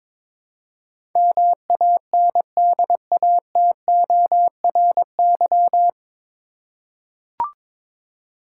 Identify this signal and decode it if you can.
Morse code, heard as a keyed tone, about 22 wpm, 700 Hz. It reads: MANDATORY
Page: Morse Code Ninja practice 22wpm